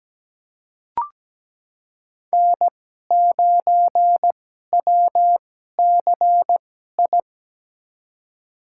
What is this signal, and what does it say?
Morse code, heard as a keyed tone, about 17 wpm, spaced as for 12 wpm, 700 Hz. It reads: N9WCI
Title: USA Call Signs - 5 Characters - 2x Character Spacing - 17wpm